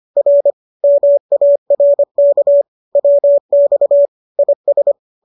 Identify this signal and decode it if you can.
Morse code, heard as a keyed tone, about 25 wpm, 590 Hz. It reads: R MARK WX IS
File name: qso206 25